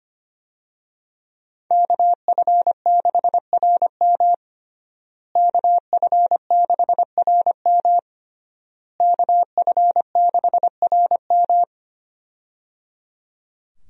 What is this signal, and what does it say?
Morse code, heard as a keyed tone, about 25 wpm, 700 Hz. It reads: KF6RM KF6RM KF6RM
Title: USA Call Signs - 5 Characters - Repeated 3x - 25wpm